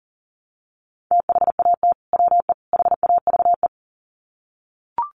Morse code, heard as a keyed tone, about 40 wpm, 700 Hz. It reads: THAT WE HAVE
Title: Sets of 3 Words 40wpm